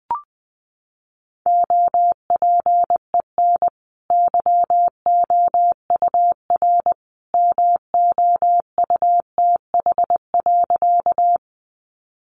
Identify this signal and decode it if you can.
Morse code, heard as a keyed tone, about 20 wpm, 700 Hz. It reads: OPEN YOUR MOUTH.
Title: Sentences from Top 1500 Words 20wpm